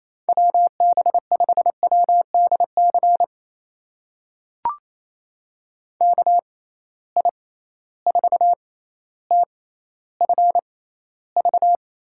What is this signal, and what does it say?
Morse code, heard as a keyed tone, about 28 wpm, spaced as for 14 wpm, 700 Hz. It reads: WB5WDC KI4TFV
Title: USA Call Signs - 6 Characters - 6x Character Spacing - 28wpm